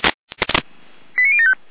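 Five-tone selective call (ZVEI), heard as quick runs of five tones, tones of about 70 ms. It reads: 89075
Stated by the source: ZVEI five-tone call 89073 received by SDR, from the Feuerwehr Orlamünde fire brigade audio archive.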